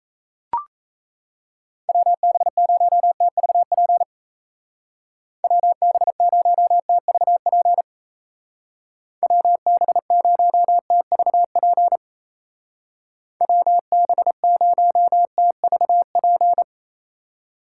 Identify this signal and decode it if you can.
Morse code, heard as a keyed tone, about 34 wpm, 700 Hz. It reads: WB0TVP WB0TVP WB0TVP WB0TVP
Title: Speed Racing - USA 2x3 Call Signs 28wpm